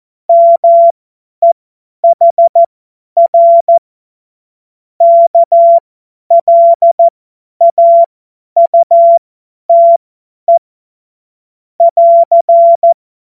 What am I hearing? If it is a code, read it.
Morse code, heard as a keyed tone, about 14 wpm, spaced as for 10 wpm, 680 Hz. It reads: MEHR KLAUTE <AR>
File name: Morsekurs / Witze / Witz 30. August 2022 50 F70